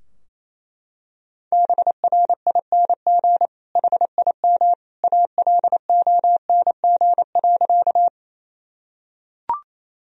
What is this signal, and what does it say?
Morse code, heard as a keyed tone, about 28 wpm, 700 Hz. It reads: BRING HIM ALONG.
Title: Sentences from Top 1000 Words - Review Part 3 - 28wpm